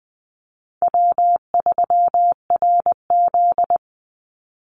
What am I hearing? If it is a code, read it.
Morse code, heard as a keyed tone, about 20 wpm, 700 Hz. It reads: W3RZ
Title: Easy Call Signs Repeated 2x - 20wpm